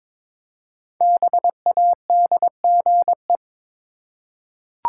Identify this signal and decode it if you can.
Morse code, heard as a keyed tone, about 22 wpm, 700 Hz. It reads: BADGE
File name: 5 Letter Words - First 3 Letters Given - 22wpm